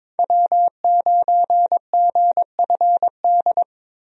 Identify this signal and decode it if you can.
Morse code, heard as a keyed tone, about 22 wpm, 700 Hz. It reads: W9GFD
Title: USA Call Signs - 5 Characters - 22wpm